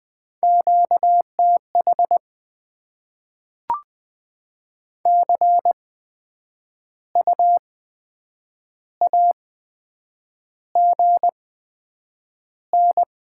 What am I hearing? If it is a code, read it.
Morse code, heard as a keyed tone, about 20 wpm, spaced as for 8 wpm, 700 Hz. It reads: QTH CU AGN
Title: Top 20 QSO Elements - 8x Character Spacing - 20wpm